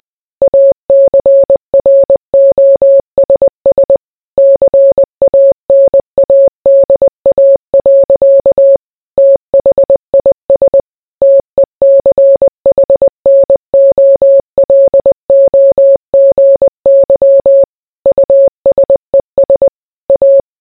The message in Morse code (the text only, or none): ACROSS CANADA. THIS TECHNOLOGY USES A